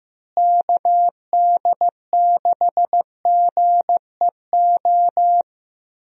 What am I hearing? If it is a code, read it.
Morse code, heard as a keyed tone, about 15 wpm, 700 Hz. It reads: KD6GEO
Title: USA Call Signs - 6 Characters - Repeated 2x - 15wpm